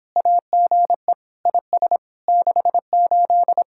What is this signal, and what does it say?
Morse code, heard as a keyed tone, about 26 wpm, 700 Hz. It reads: AGE IS 68